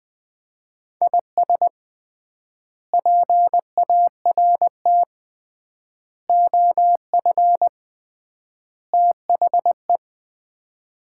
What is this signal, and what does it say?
Morse code, heard as a keyed tone, about 20 wpm, 700 Hz. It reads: IS PART OF THE
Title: 3x Word-Spacing - Sets of 4 Words - 20wpm